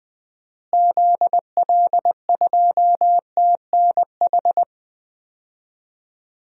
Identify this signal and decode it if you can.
Morse code, heard as a keyed tone, about 20 wpm, 700 Hz. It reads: ZL2TNH
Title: DXCC-100 Call Signs - 6 Characters - Repeated 2x - 20wpm